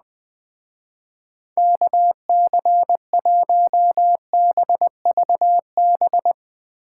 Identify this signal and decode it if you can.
Morse code, heard as a keyed tone, about 20 wpm, 700 Hz. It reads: KC1BVB
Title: Speed Racing - USA Call Signs 17wpm